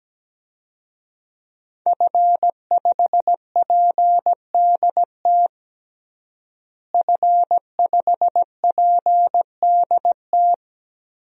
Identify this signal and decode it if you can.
Morse code, heard as a keyed tone, about 17 wpm, 700 Hz. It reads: F5PDT F5PDT
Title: DXCC-100 Call Signs - 5 Characters - Repeated 2x - 17wpm